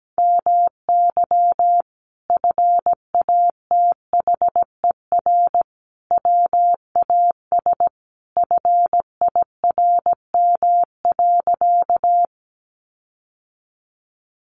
Morse code, heard as a keyed tone, about 17 wpm, 700 Hz. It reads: MY FATHER WAS FIRM.
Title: Sentences from Top 1400 Words 17wpm